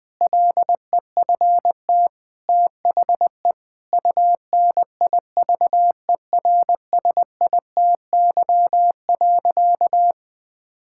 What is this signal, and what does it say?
Morse code, heard as a keyed tone, about 20 wpm, 700 Hz. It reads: LEFT THE UNIVERSITY.
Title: Sentences from Top 1000 Words - Review Part 5 - 20wpm